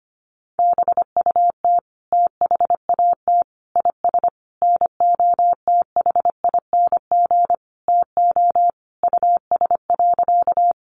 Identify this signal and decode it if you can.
Morse code, heard as a keyed tone, about 25 wpm, 700 Hz. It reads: BUT THAT IS NOTHING TO US.
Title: Sentences from Top 500 Words 25wpm